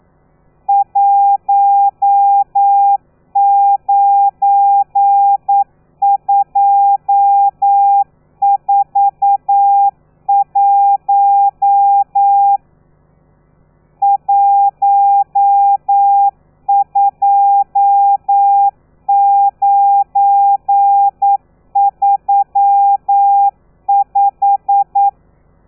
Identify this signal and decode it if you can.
Morse code, heard as a keyed tone, about 9 wpm, 800 Hz. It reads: 19241 12935
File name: telegrafia / i2tfj / Lez12 40